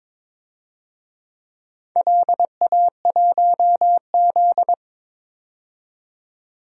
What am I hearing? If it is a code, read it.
Morse code, heard as a keyed tone, about 22 wpm, 700 Hz. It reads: LA1Z
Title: DXCC-100 Call Signs - 4 Characters - 22wpm